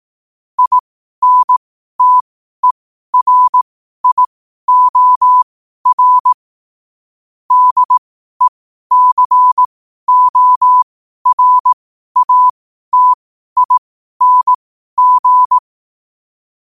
Morse code, heard as a keyed tone, about 18 wpm, spaced as for 12 wpm, 1000 Hz. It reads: INTERIOR DECORATING
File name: qotdfiles / 12wpm